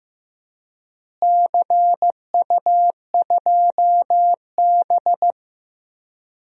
Morse code, heard as a keyed tone, about 15 wpm, 700 Hz. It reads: CU2B
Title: DXCC-100 Call Signs - 4 Characters - Repeated 3x - 15wpm